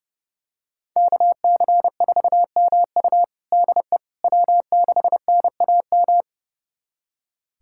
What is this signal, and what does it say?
Morse code, heard as a keyed tone, about 30 wpm, 700 Hz. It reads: KC4MU DE W6NAM
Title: USA Callsign DE Callsign - 5 Characters - 30wpm